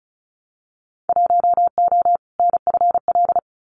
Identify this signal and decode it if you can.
Morse code, heard as a keyed tone, about 36 wpm, 700 Hz. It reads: 1O NFL
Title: Winter Field Day Exchanges 35wpm